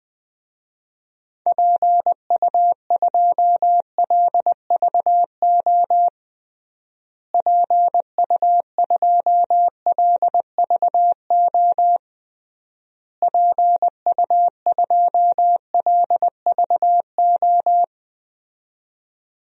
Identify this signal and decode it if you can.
Morse code, heard as a keyed tone, about 20 wpm, 700 Hz. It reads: PU2LVO PU2LVO PU2LVO
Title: DXCC-100 Call Signs - 6 Characters - Repeated 3x - 20wpm